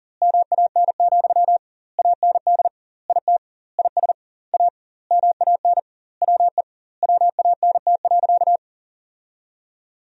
Morse code, heard as a keyed tone, about 37 wpm, 700 Hz. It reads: MAN, AND IT IS A MAN WE WANT.
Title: 2x Word-Spacing - Sentences from Top 200 Words - 40wpm